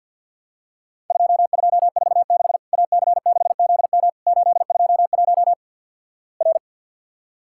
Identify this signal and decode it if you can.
Morse code, heard as a keyed tone, about 54 wpm, 700 Hz. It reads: 2246 AC6ZM 821 R
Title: Worked All Europe DX Contest - QTC Exchange with Repeats - 50wpm